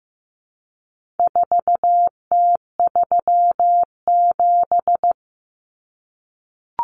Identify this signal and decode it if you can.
Morse code, heard as a keyed tone, about 15 wpm, 700 Hz. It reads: 4T37
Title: Common Cut Numbers - 4-Digit 15wpm